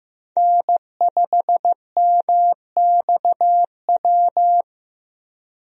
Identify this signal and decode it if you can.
Morse code, heard as a keyed tone, about 15 wpm, 700 Hz. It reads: N5MXW
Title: USA Call Signs - 5 Characters - Repeated 3x - 15wpm